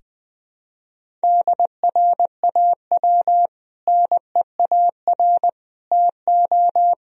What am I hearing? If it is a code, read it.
Morse code, heard as a keyed tone, about 21 wpm, 700 Hz. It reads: DRAW NEAR TO
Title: Sentences from Top 1000 Words - Review Part 1 - 20wpm